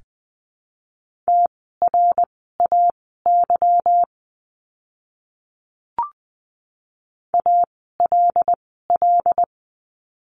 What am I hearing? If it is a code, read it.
Morse code, heard as a keyed tone, about 20 wpm, spaced as for 14 wpm, 700 Hz. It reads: TRAY ALL
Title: 146 Lesson 24f - Letter y - Words 20wpm